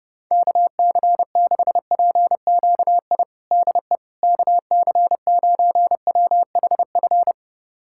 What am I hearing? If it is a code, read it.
Morse code, heard as a keyed tone, about 32 wpm, 700 Hz. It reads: KC6PQI DE KC9WHF
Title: USA Callsign DE Callsign - 6 Characters - 30wpm